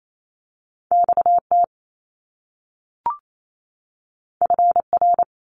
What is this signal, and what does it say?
Morse code, heard as a keyed tone, about 28 wpm, 700 Hz. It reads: XT FR
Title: Top Two Letter Combinations in Words 28wpm